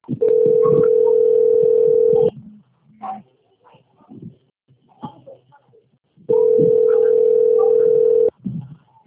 Ringback tone (North America), 440 + 480 Hz.
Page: Recife - PE